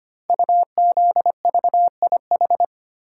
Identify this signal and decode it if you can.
Morse code, heard as a keyed tone, about 25 wpm, 690 Hz. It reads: UZVIH
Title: Code Groups - 5 Letters - 25wpm